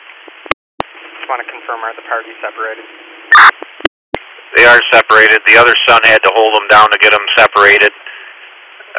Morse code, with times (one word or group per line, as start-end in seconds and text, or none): none